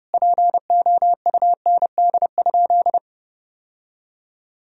Morse code, heard as a keyed tone, about 30 wpm, 700 Hz. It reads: POUND?
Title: Sentences from Top 1000 Words - Review Part 2 - 30wpm